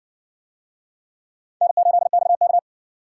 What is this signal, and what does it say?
Morse code, heard as a keyed tone, about 58 wpm, 700 Hz. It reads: N8XK